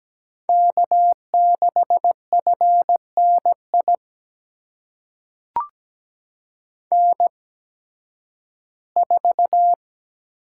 Morse code, heard as a keyed tone, about 17 wpm, spaced as for 11 wpm, 700 Hz. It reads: K6FNI N4
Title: USA Call Signs - 5 Characters - 8x Character Spacing - 17wpm